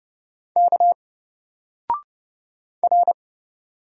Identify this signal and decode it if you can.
Morse code, heard as a keyed tone, about 30 wpm, 700 Hz. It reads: K R
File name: Single-Letter-Number-30wpm